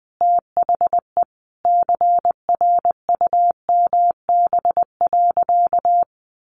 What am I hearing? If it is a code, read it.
Morse code, heard as a keyed tone, about 20 wpm, 700 Hz. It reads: THE CRUMB.
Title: Easy Phrases 20wpm